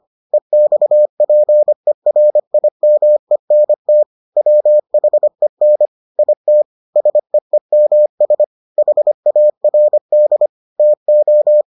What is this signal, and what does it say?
Morse code, heard as a keyed tone, about 25 wpm, 600 Hz. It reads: EXPERIMENT WHEN IT SEEMS HARD TO